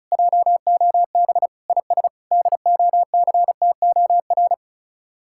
Morse code, heard as a keyed tone, about 47 wpm, spaced as for 41 wpm, 700 Hz. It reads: JOB IS DOCTOR
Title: Speed-Racing Common QSO Phrases - 35wpm